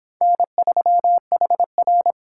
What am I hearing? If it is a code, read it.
Morse code, heard as a keyed tone, about 29 wpm, 700 Hz. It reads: N3HR